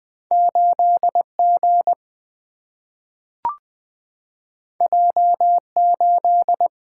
Morse code, heard as a keyed tone, about 20 wpm, 700 Hz. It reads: 8 G J 8
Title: Single-Letter Single-Number 20wpm